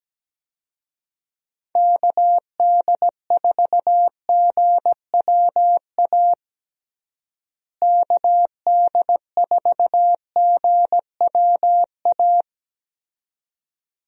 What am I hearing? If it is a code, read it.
Morse code, heard as a keyed tone, about 17 wpm, 700 Hz. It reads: KD4GWA KD4GWA
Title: USA Call Signs - 6 Characters - Repeated 2x - 17wpm